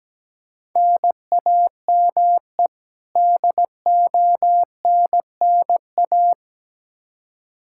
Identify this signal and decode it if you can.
Morse code, heard as a keyed tone, about 17 wpm, 700 Hz. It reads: NAME DONNA
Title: Practice QSO - Exchange 1 - Beginning - 17wpm